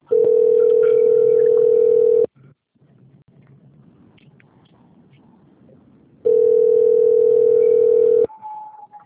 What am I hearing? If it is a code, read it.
Ringback tone (North America), 440 + 480 Hz.